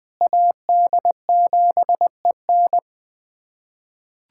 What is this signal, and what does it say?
Morse code, heard as a keyed tone, about 20 wpm, 700 Hz. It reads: AD7EN
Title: USA Callsign DE Callsign - 5 Characters - 20wpm